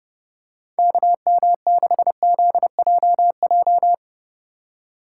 Morse code, heard as a keyed tone, about 30 wpm, 700 Hz. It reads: KM6ZJJ